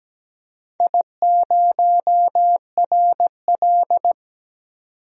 Morse code, heard as a keyed tone, about 17 wpm, 700 Hz. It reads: I0RL
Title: DXCC-100 Call Signs - 4 Characters - 17wpm